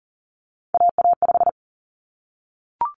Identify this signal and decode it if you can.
Morse code, heard as a keyed tone, about 40 wpm, 700 Hz. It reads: AA5
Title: Common Cut Numbers - 3-Digit 40wpm